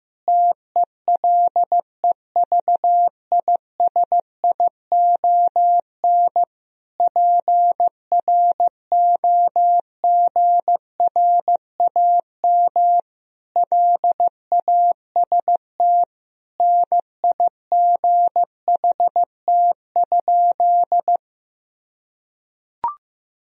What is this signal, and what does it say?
Morse code, heard as a keyed tone, about 15 wpm, 700 Hz. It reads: TELEVISION PROGRAM LAST NIGHT?
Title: Sentences from Top 1000 Words - Review Part 4 - 15wpm